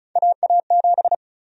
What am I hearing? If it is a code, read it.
Morse code, heard as a keyed tone, about 36 wpm, 700 Hz. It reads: AA7
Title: USA Call Sign Prefixes - 35wpm